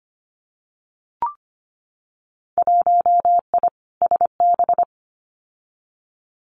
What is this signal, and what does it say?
Morse code, heard as a keyed tone, about 25 wpm, 700 Hz. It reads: 1I SB
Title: Winter Field Day Exchanges 25wpm